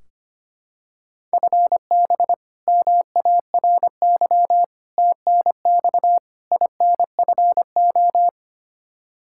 Morse code, heard as a keyed tone, about 25 wpm, 700 Hz. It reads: FB MARY TNX INFO
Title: Practice QSO - Exchange 3 - More Info - 25wpm